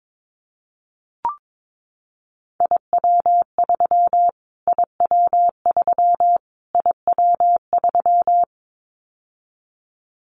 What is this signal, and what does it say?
Morse code, heard as a keyed tone, about 22 wpm, 700 Hz. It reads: IW3 IW3 IW3
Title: Callsign Prefixes for Top 25 DXCC - Repeated 3x - 22wpm